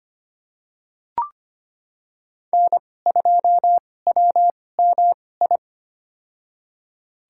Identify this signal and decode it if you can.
Morse code, heard as a keyed tone, about 25 wpm, spaced as for 18 wpm, 700 Hz. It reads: N2WMI
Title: USA Call Signs - 5 Characters - 2x Character Spacing - 25wpm